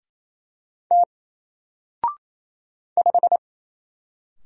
Morse code, heard as a keyed tone, about 27 wpm, 700 Hz. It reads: T 5
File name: Single-Letter-Number-28wpm